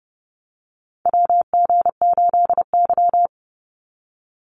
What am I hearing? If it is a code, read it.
Morse code, heard as a keyed tone, about 30 wpm, 700 Hz. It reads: WG8Y
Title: NA SOTA Chaser Callsigns 30wpm